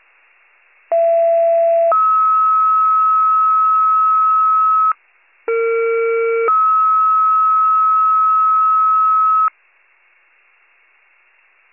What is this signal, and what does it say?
Two-tone page, heard as a one-second tone, then a three-second tone, about 670 then 1290 Hz, about 460 then 1290 Hz. It is heard from 0.9 to 9.5 s.